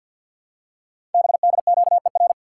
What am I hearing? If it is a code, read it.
Morse code, heard as a keyed tone, about 48 wpm, 700 Hz. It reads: BDQER
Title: Code Groups - 5 Letters - 50wpm